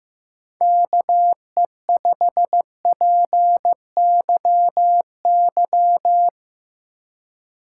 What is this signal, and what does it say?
Morse code, heard as a keyed tone, about 17 wpm, 700 Hz. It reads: KE5PYY
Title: Speed Racing - USA 2x3 Call Signs 15wpm